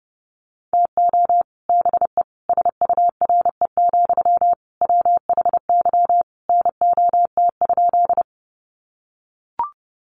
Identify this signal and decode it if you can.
Morse code, heard as a keyed tone, about 30 wpm, 700 Hz. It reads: TO BE SURE, WHY NOT?
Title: Sentences from Top 600 Words 30wpm